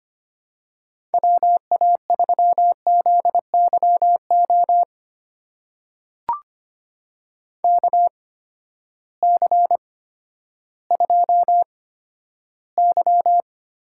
Morse code, heard as a keyed tone, about 25 wpm, spaced as for 12 wpm, 700 Hz. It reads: WA3ZYO KC2Y
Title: USA Call Signs - 6 Characters - 8x Character Spacing - 25wpm